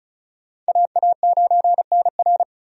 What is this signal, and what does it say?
Morse code, heard as a keyed tone, about 36 wpm, 700 Hz. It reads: AA9NR
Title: USA Call Signs - 5 Characters - 35wpm